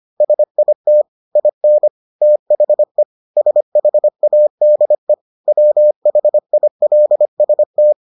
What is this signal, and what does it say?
Morse code, heard as a keyed tone, about 25 wpm, 600 Hz. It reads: SIT IN THE SHADE WHILST